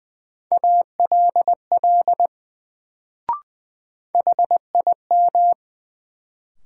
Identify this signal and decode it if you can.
Morse code, heard as a keyed tone, about 20 wpm, 700 Hz. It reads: ALL HIM
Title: Sets of 2 Words - New Word Drill - 20wpm